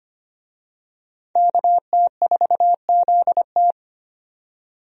Morse code, heard as a keyed tone, about 25 wpm, 700 Hz. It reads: KT4ZT
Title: USA Call Signs - 5 Characters - 25wpm